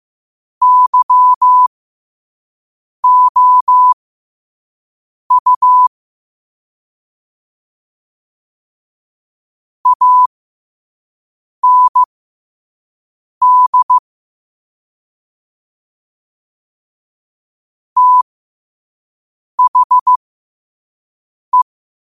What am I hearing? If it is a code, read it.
Morse code, heard as a keyed tone, about 15 wpm, spaced as for 5 wpm, 1000 Hz. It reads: YOU AND THE
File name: qotdfiles / 5wpm